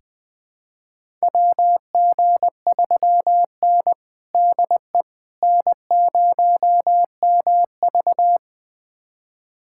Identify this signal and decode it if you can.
Morse code, heard as a keyed tone, about 20 wpm, 700 Hz. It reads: WG3N DE N0MV
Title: USA Callsign DE Callsign - 4 Characters - 20wpm